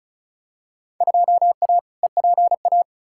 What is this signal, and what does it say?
Morse code, heard as a keyed tone, about 34 wpm, 700 Hz. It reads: 2A EPA
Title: Field Day Exchanges - 35wpm